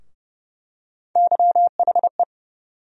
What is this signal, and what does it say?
Morse code, heard as a keyed tone, about 30 wpm, 700 Hz. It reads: YHE
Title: USA Call Sign Postfixes - 3 Letters 30wpm